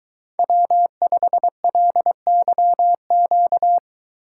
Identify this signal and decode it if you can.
Morse code, heard as a keyed tone, about 23 wpm, 700 Hz. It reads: W5LYQ